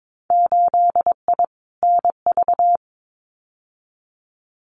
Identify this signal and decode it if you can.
Morse code, heard as a keyed tone, about 22 wpm, 700 Hz. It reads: 8I NV
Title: Winter Field Day Exchanges Encore 22wpm